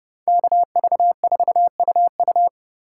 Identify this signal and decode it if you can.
Morse code, heard as a keyed tone, about 30 wpm, 700 Hz. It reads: KV4UU